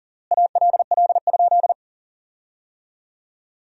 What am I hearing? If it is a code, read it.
Morse code, heard as a keyed tone, about 40 wpm, 700 Hz. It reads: ALL?
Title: 3x Word-Spacing - Sentences from Top 200 Words - 40wpm